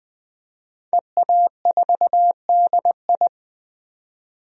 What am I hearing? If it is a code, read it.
Morse code, heard as a keyed tone, about 20 wpm, 700 Hz. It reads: EA4DI
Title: DXCC-100 Call Signs - 5 Characters - 20wpm